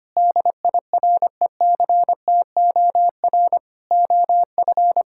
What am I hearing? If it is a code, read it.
Morse code, heard as a keyed tone, about 25 wpm, 700 Hz. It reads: DIRECTOR OF
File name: practice / 25wpm / Sentences with 3-Word Phrases - 25wpm